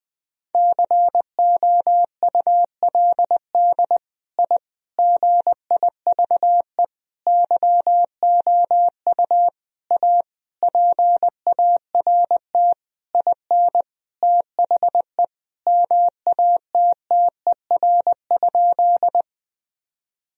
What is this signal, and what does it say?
Morse code, heard as a keyed tone, about 20 wpm, 700 Hz. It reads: COULD I GIVE YOU A PART IN THE MATTER?
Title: Sentences from Top 1000 Words - Review Part 1 - 20wpm